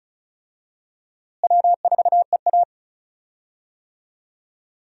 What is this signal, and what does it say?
Morse code, heard as a keyed tone, about 36 wpm, 700 Hz. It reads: W4EA
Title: USA Call Signs - 4 Characters - Repeated 3x - 35wpm